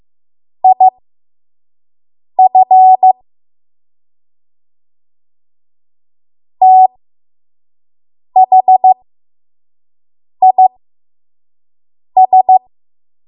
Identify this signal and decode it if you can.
Morse code, heard as a keyed tone, about 15 wpm, spaced as for 5 wpm, 750 Hz. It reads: IF THIS
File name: Morse / 05WPM